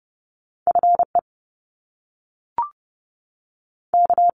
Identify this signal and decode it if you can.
Morse code, heard as a keyed tone, about 30 wpm, 700 Hz. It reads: F E K
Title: Top Two Letter Combinations in Words 30wpm